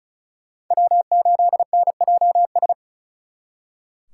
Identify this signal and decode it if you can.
Morse code, heard as a keyed tone, about 47 wpm, spaced as for 41 wpm, 700 Hz. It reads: W8NJS
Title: Speed Racing - USA 1x3 Call Signs 35wpm